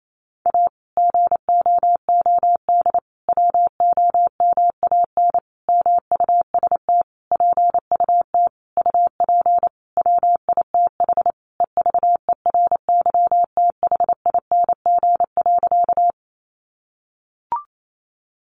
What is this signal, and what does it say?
Morse code, heard as a keyed tone, about 28 wpm, 700 Hz. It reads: A GOOD WOMAN MUST PUT UP WITH EVERYTHING.
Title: Sentences from Top 1100 Words 28wpm